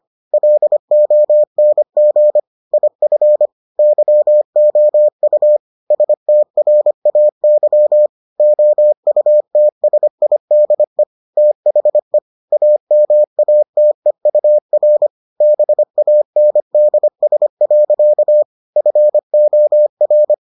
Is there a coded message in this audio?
Yes: Morse code from 0.3 to 20.3 s, heard as a keyed tone, about 25 wpm, 600 Hz. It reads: LONG IF YOU STRAY OUTSIDE THE AMATEUR BANDS. FOR